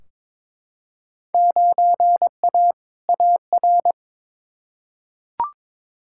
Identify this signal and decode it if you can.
Morse code, heard as a keyed tone, about 22 wpm, 700 Hz. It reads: 9A AR
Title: Field Day Exchanges Encore - 22wpm